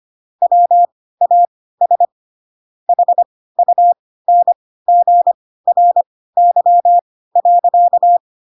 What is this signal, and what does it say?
Morse code, heard as a keyed tone, about 25 wpm, spaced as for 16 wpm, 700 Hz. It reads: WAS HUNGRY.
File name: Pinocchio / Pinocchio ch7 10 25 ik1hge